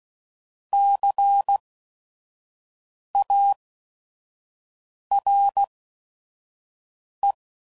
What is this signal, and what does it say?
Morse code, heard as a keyed tone, about 16 wpm, spaced as for 5 wpm, 800 Hz. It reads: CARE
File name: Prefixes / By First Letter / Cs